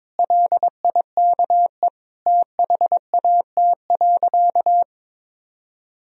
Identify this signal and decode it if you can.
Morse code, heard as a keyed tone, about 22 wpm, 700 Hz. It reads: LIKE THAT.
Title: Sentences from Top 1000 Words - Review Part 2 - 22wpm